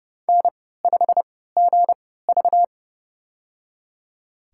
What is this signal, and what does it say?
Morse code, heard as a keyed tone, about 31 wpm, spaced as for 17 wpm, 700 Hz. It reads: N5GV
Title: USA Call Signs - 4 Characters - 3x Character Spacing - 30wpm